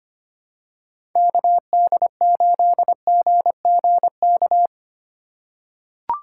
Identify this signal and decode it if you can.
Morse code, heard as a keyed tone, about 25 wpm, 700 Hz. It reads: KD8GGK
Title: USA Call Signs - 6 Characters - 3x Character Spacing - 25wpm